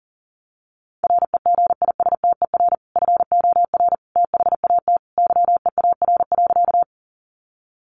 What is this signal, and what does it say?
Morse code, heard as a keyed tone, about 40 wpm, 700 Hz. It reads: REGISTER FOR THAT YEAR.
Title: Sentences from Top 1200 Words 40wpm